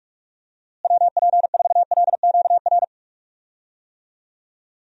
Morse code, heard as a keyed tone, about 44 wpm, 700 Hz. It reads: WP4LQR
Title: USA Call Signs - 6 Characters - Repeated 3x - 45wpm